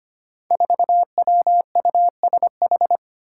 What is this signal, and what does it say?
Morse code, heard as a keyed tone, about 25 wpm, 700 Hz. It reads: 4WUSH
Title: Five Challenging Letter-Number Combinations 25wpm